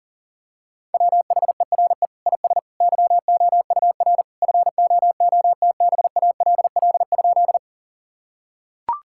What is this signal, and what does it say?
Morse code, heard as a keyed tone, about 40 wpm, 700 Hz. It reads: WHERE IS YOUR FOOTBALL?
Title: Sentences from Top 1600 Words 40wpm